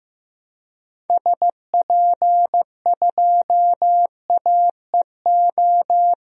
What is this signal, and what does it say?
Morse code, heard as a keyed tone, about 15 wpm, 700 Hz. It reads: SP2AEO
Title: DXCC-100 Call Signs - 6 Characters - 15wpm